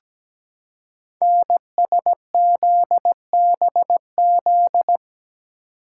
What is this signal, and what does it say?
Morse code, heard as a keyed tone, about 17 wpm, 700 Hz. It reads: NSZBZ
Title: Code Groups - 5 Letters - 17wpm